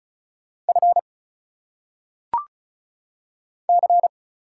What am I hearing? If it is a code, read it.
Morse code, heard as a keyed tone, about 33 wpm, 700 Hz. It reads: F C
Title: Single Letters - Phonetic Pronunciation 35wpm